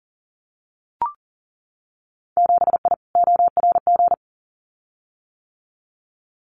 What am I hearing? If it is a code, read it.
Morse code, heard as a keyed tone, about 40 wpm, 700 Hz. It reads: 7I ORG
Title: Winter Field Day Exchanges Encore 40wpm